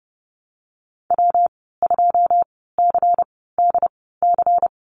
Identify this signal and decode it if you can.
Morse code, heard as a keyed tone, about 30 wpm, spaced as for 17 wpm, 700 Hz. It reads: W2CDC
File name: Lesson 19g - Letter w - Call Signs 30wpm